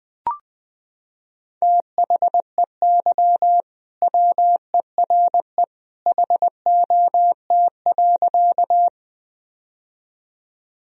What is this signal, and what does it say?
Morse code, heard as a keyed tone, about 20 wpm, 700 Hz. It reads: THEY WERE HOT.
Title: Sentences from Top 1000 Words - Review Part 3 - 20wpm